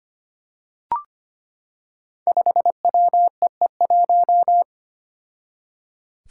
Morse code, heard as a keyed tone, about 25 wpm, 700 Hz. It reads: 5WEE1
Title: Five Challenging Letter-Number Combinations 25wpm